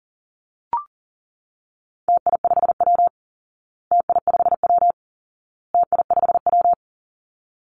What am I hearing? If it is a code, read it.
Morse code, heard as a keyed tone, about 40 wpm, 700 Hz. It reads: TI5W TI5W TI5W
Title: Call Signs Repeated 2x - 40wpm